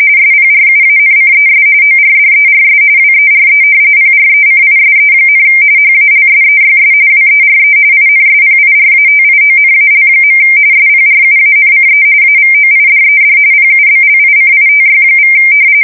RTTY, about 45.45 baud. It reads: SPRUANCES ORDERS WERE PRECISE, HALSEYS WERE INSPIRATIONAL. SPRUANCE AVOIDED PUBLICITY=
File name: fsk170 / battleshipnj